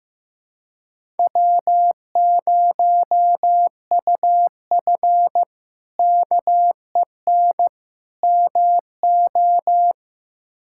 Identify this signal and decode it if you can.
Morse code, heard as a keyed tone, about 15 wpm, 700 Hz. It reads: W0UF KEN MO
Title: K1USN SST QSO Practice - 15wpm